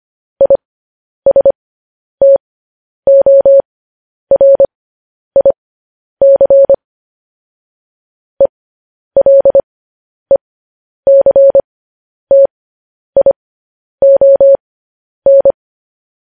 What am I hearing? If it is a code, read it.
Morse code, heard as a keyed tone, about 25 wpm, spaced as for 10 wpm, 550 Hz. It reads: ISTORIC ELECTION
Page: news headlines in morse code at 10 wpm